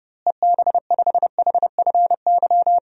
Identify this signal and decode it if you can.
Morse code, heard as a keyed tone, about 30 wpm, 700 Hz. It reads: EB5HFY
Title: DXCC-100 Call Signs - 6 Characters - Repeated 2x - 30wpm